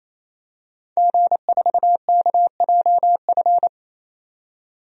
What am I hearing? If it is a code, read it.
Morse code, heard as a keyed tone, about 28 wpm, 700 Hz. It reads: G4KJF
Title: DXCC-100 Call Signs - 5 Characters - 28wpm